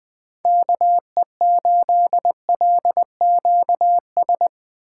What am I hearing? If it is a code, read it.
Morse code, heard as a keyed tone, about 22 wpm, 700 Hz. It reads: KE8LQS
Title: Speed Racing - USA 2x3 Call Signs 20wpm